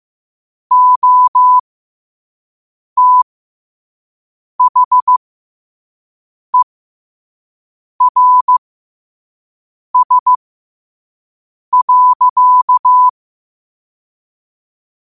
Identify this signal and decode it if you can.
Morse code, heard as a keyed tone, about 15 wpm, spaced as for 5 wpm, 1000 Hz. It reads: OTHERS.
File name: qotdfiles / 5wpm